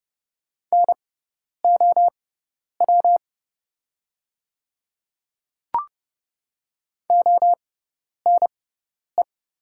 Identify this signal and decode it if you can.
Morse code, heard as a keyed tone, about 30 wpm, spaced as for 10 wpm, 700 Hz. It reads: NOW ONE
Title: 6x Character Spacing - Top 100 Words - 30wpm